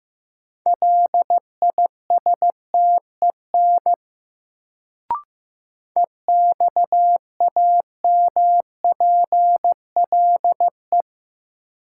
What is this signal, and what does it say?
Morse code, heard as a keyed tone, about 15 wpm, 700 Hz. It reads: LISTEN EXAMPLE
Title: Sentences from Top 400 Words - New Word Drill - 15wpm